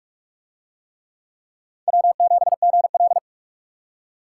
Morse code, heard as a keyed tone, about 44 wpm, 700 Hz. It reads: W7GL